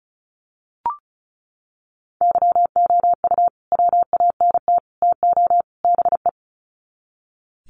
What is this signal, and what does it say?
Morse code, heard as a keyed tone, about 34 wpm, 700 Hz. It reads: YOU WANT TO BE
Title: Sets of 4 Words 35wpm